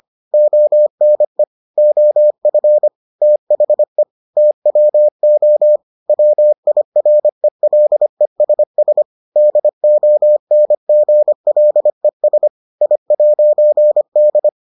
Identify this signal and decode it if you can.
Morse code, heard as a keyed tone, about 25 wpm, 600 Hz. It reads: ONE OF THE TWO WIRELESS DONGLES I'D